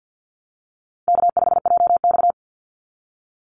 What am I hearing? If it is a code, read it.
Morse code, heard as a keyed tone, about 49 wpm, 700 Hz. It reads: K5JX
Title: Call Signs Repeated 2x - 50wpm